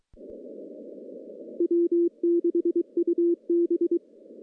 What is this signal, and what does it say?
Morse code, heard as a keyed tone, about 23 wpm, 340 Hz. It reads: W6UB
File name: contests / CW-SS-k5tr- -1500